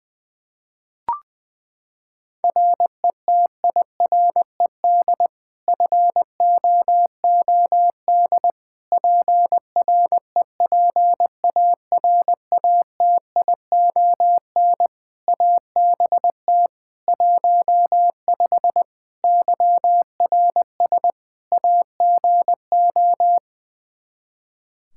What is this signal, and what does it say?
Morse code, heard as a keyed tone, about 20 wpm, 700 Hz. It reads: RETIRED FOOD PREPARATION ABT 15 YRS AGO
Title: K1USN SST QSO Practice 20wpm